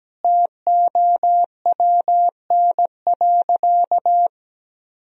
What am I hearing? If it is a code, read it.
Morse code, heard as a keyed tone, about 17 wpm, 700 Hz. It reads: TOWN.
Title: Sentences from Top 1000 Words - Review Part 2 - 17wpm